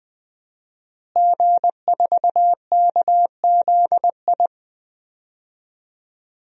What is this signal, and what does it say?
Morse code, heard as a keyed tone, about 20 wpm, 700 Hz. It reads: G4KZI
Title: DXCC-100 Call Signs - 5 Characters - Repeated 2x - 20wpm